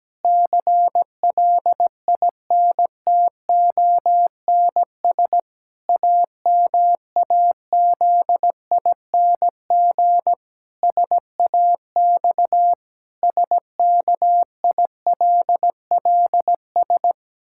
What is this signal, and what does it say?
Morse code, heard as a keyed tone, about 17 wpm, 700 Hz. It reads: CLINTONS AMAZING SAX SKILLS
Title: Pangram Challenge v2 - 17wpm